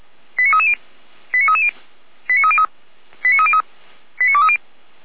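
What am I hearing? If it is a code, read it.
Five-tone selective call (ZVEI), heard as quick runs of five tones, tones of about 70 ms. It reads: 89330 89330 89303 89303 89230
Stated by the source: Five-tone ZVEI selective call received by SDR, from Feuerwehr Orlamünde.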